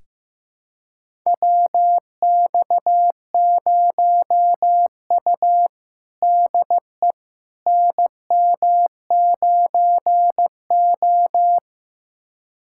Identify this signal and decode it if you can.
Morse code, heard as a keyed tone, about 15 wpm, 700 Hz. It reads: WX0U DE NM9O
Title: USA Callsign DE Callsign - 4 Characters - 15wpm